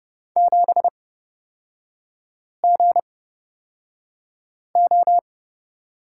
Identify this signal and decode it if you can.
Morse code, heard as a keyed tone, about 31 wpm, spaced as for 5 wpm, 700 Hz. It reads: 7GO
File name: Lesson 38e - Letter q - Three Characters 30wpm